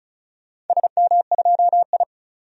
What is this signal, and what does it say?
Morse code, heard as a keyed tone, about 36 wpm, 700 Hz. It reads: SM2I
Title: DXCC-100 Call Signs - 4 Characters - 35wpm